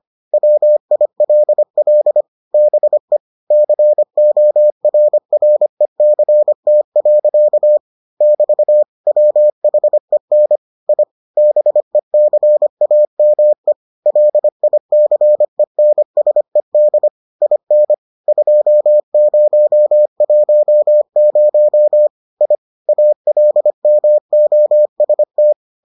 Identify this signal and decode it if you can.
Morse code, heard as a keyed tone, about 25 wpm, 600 Hz. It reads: WILL BE CORRECT. <BT> WHEN I BECAME LICENSED IN 2010 I ALMOST